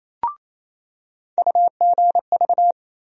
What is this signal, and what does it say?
Morse code, heard as a keyed tone, about 28 wpm, 700 Hz. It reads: UGV
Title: USA Call Sign Postfixes - 3 Letters 28wpm